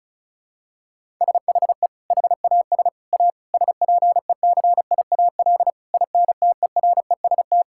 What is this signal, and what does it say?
Morse code, heard as a keyed tone, about 36 wpm, 700 Hz. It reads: SHE HAS A SPECIAL INTEREST
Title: Sentences from Top 1000 Words - Review Part 5 - 35wpm